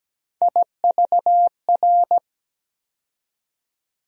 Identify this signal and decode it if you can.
Morse code, heard as a keyed tone, about 17 wpm, 700 Hz. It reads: IVR
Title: USA Call Sign Postfixes - 3 Letters 17wpm